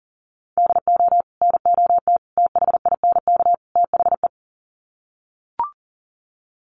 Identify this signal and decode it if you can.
Morse code, heard as a keyed tone, about 40 wpm, 700 Hz. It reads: DO NOT THINK THE
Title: Sets of 4 Words 40wpm